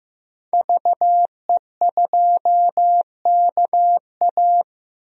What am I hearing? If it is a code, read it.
Morse code, heard as a keyed tone, about 15 wpm, 700 Hz. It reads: VE2KA
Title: DXCC-100 Call Signs - 5 Characters - Repeated 2x - 15wpm